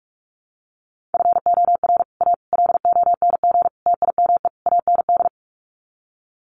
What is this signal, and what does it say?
Morse code, heard as a keyed tone, about 44 wpm, 700 Hz. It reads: FOR A LONG TIME AND
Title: Sets of 5 Words Encore2 45wpm